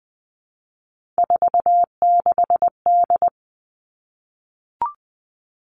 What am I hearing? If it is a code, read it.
Morse code, heard as a keyed tone, about 20 wpm, 700 Hz. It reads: 46D
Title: Three Challenging Letter-Number Combinations 20wpm